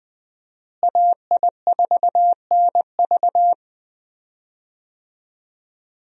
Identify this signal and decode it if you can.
Morse code, heard as a keyed tone, about 20 wpm, 700 Hz. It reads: AI4NV
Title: USA Call Signs - 5 Characters - Repeated 2x - 20wpm